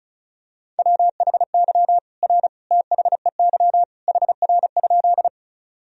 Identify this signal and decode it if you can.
Morse code, heard as a keyed tone, about 36 wpm, 700 Hz. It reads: WHY R THEY HR?
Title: Sentences from Top 200 Words and Abbreviations 35wpm